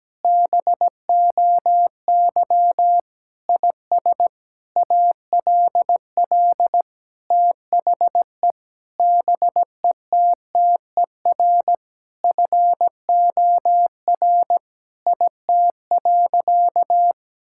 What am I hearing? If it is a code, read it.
Morse code, heard as a keyed tone, about 17 wpm, 700 Hz. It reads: BOY IS ALL THE BETTER FOR IT.
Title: Sentences from Top 1000 Words - Review Part 5 - 17wpm